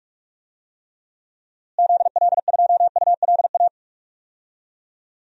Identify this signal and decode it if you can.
Morse code, heard as a keyed tone, about 47 wpm, 700 Hz. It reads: ZL2ULA
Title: DXCC-100 Call Signs - 6 Characters - Repeated 2x - 45wpm